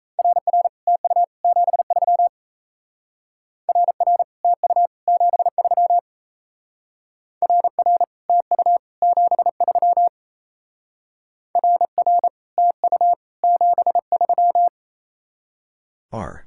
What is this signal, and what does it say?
Morse code, heard as a keyed tone, about 37 wpm, 700 Hz. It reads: RR TU 73 RR TU 73 RR TU 73 RR TU 73